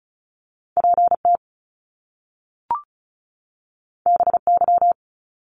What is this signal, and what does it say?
Morse code, heard as a keyed tone, about 36 wpm, 700 Hz. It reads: PT BY
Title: Top Two Letter Combinations in Words 35wpm